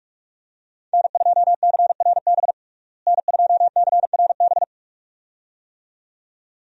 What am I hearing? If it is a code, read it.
Morse code, heard as a keyed tone, about 42 wpm, 700 Hz. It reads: N2CRB N2CRB
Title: USA Call Signs - 5 Characters - Repeated 2x - 45wpm